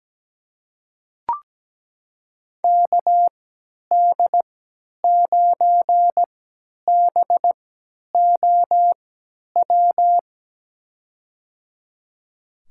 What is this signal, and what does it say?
Morse code, heard as a keyed tone, about 17 wpm, spaced as for 10 wpm, 700 Hz. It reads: KD9BOW
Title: USA Call Signs - 6 Characters - 3x Character Spacing - 17wpm